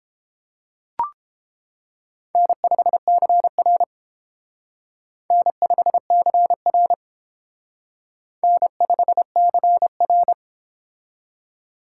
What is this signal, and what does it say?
Morse code, heard as a keyed tone, about 28 wpm, 700 Hz. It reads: N5CR N5CR N5CR